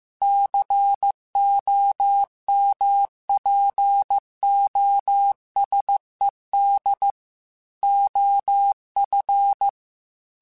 Morse code, heard as a keyed tone, about 15 wpm, 800 Hz. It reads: COMPOSED OF
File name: Technical Subjects / Technical Subjects-mp3s / Sw Pwr Sup @13